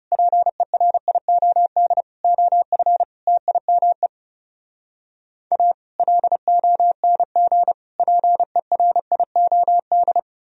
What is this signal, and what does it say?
Morse code, heard as a keyed tone, about 37 wpm, 700 Hz. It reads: PERIOD OF TIME A LONG PERIOD
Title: Speed-Racing - Sets of 5 Words 30wpm